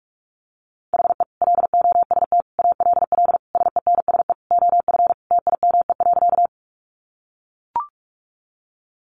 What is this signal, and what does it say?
Morse code, heard as a keyed tone, about 44 wpm, 700 Hz. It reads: HE LOST ALL SENSE OF TIME.
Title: Sentences from Top 1400 Words 45wpm